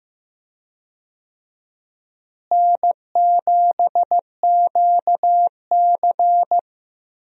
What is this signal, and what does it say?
Morse code, heard as a keyed tone, about 15 wpm, 700 Hz. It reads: N7QC
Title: USA Call Signs - 4 Characters - Repeated 2x - 15wpm